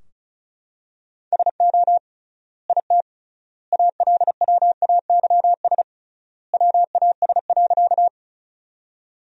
Morse code, heard as a keyed tone, about 36 wpm, 700 Hz. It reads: SO IT ALWAYS WAS.
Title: 3x Word-Spacing - Sentences from Top 300 Words - 35wpm